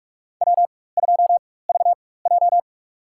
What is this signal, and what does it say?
Morse code, heard as a keyed tone, about 43 wpm, spaced as for 21 wpm, 700 Hz. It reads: W2VJ
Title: USA Call Signs - 4 Characters - 4x Character Spacing - 45wpm